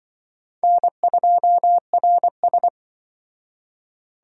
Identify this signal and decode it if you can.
Morse code, heard as a keyed tone, about 26 wpm, 700 Hz. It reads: N2RS